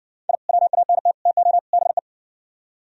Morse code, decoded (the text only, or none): I WANT TO BE